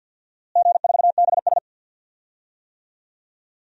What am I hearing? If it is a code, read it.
Morse code, heard as a keyed tone, about 48 wpm, 700 Hz. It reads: G4BS
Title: DXCC-100 Call Signs - 4 Characters - Repeated 2x - 50wpm